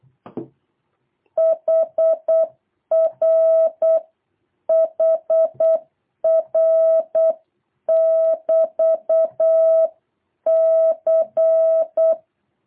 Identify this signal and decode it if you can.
Morse code, heard as a keyed tone, about 8 wpm, 650 Hz. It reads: HR HR <BT> C
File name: jh8cbh / 2023cw